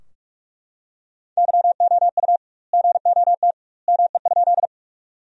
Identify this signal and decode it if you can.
Morse code, heard as a keyed tone, about 43 wpm, 700 Hz. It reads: YOU GOT ME?
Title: 2x Word-Spacing - Sentences from Top 300 Words - 45wpm